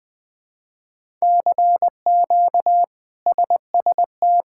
Morse code, heard as a keyed tone, about 21 wpm, 700 Hz. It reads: CQ SST
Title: K1USN SST QSO Practice - 20wpm